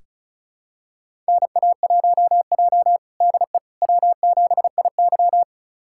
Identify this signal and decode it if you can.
Morse code, heard as a keyed tone, about 36 wpm, 700 Hz. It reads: NA1J DE W7IY
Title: USA Callsign DE Callsign - 4 Characters - 35wpm